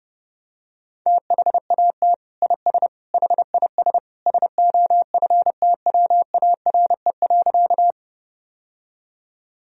Morse code, heard as a keyed tone, about 32 wpm, 700 Hz. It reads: THAT IS HIS SOFTWARE.
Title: Sentences from Top 1000 Words - Review Part 3 - 30wpm